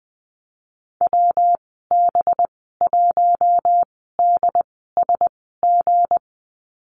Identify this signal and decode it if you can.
Morse code, heard as a keyed tone, about 20 wpm, spaced as for 15 wpm, 700 Hz. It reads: WB1DSG
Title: 213 Lesson 34g - Number 8 - Call Signs 20wpm